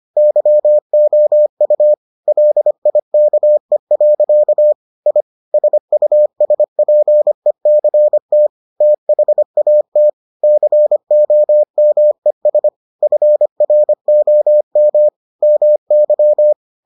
Morse code, heard as a keyed tone, about 25 wpm, 600 Hz. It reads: YOU LIKE. I SUSPECT THAT COMES FROM MY